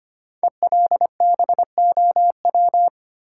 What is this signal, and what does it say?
Morse code, heard as a keyed tone, about 25 wpm, 700 Hz. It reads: ELBOW
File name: 5 Letter Words - First 3 Letters Given - 25wpm